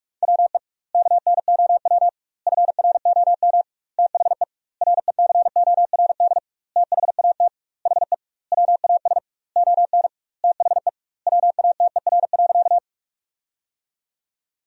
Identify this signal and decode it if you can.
Morse code, heard as a keyed tone, about 41 wpm, 700 Hz. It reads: WE KNOW FROM THE RECORD THAT HE WAS ON THE WATER.
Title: 2x Word-Spacing - Sentences from Top 500 Words - 45wpm